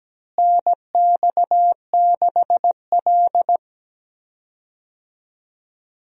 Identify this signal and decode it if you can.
Morse code, heard as a keyed tone, about 17 wpm, 700 Hz. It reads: NX6L
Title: USA Call Signs - 4 Characters - Repeated 2x - 17wpm